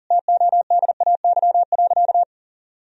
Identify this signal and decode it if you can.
Morse code, heard as a keyed tone, about 37 wpm, 700 Hz. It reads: TODAY.
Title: Sentences from Top 1000 Words - Review Part 4 - 40wpm